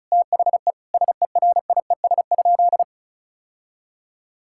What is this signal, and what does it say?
Morse code, heard as a keyed tone, about 34 wpm, 700 Hz. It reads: THE SERIES?
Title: Sentences from Top 1000 Words - Review Part 5 - 35wpm